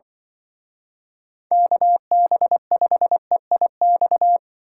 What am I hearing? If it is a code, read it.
Morse code, heard as a keyed tone, about 25 wpm, 700 Hz. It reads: KB5EIX
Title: Speed Racing - USA 2x3 Call Signs 20wpm